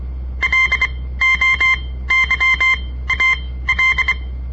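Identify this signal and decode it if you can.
Morse code, heard as a keyed tone, about 24 wpm, spaced as for 16 wpm, 2100 Hz. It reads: LOYAL
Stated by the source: On-air Morse code practice broadcast by a southern Montana amateur radio association; all code, no read back.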